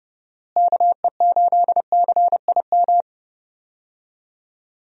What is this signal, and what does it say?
Morse code, heard as a keyed tone, about 31 wpm, 700 Hz. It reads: KE8CIM
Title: USA Call Signs - 6 Characters - Repeated 3x - 30wpm